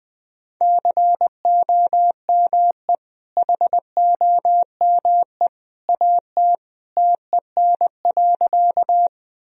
Morse code, heard as a keyed tone, about 20 wpm, 700 Hz. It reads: COME HOME AT TEN.
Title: Sentences from Top 1000 Words - Review Part 2 - 20wpm